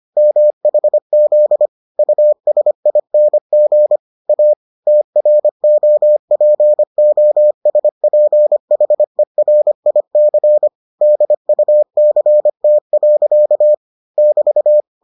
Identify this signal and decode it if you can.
Morse code, heard as a keyed tone, about 25 wpm, 600 Hz. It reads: MHZ USING A TROPOSPHERIC DUCT. <BT>